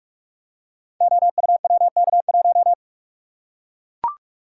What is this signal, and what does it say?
Morse code, heard as a keyed tone, about 44 wpm, 700 Hz. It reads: OUWK1
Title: Code Groups - 5 Letter:Numbers - 45wpm